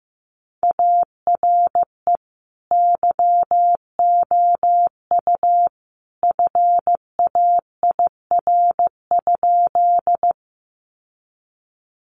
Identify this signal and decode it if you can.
Morse code, heard as a keyed tone, about 15 wpm, 700 Hz. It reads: ARE YOU FAIR?
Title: Sentences from Top 1300 Words 15wpm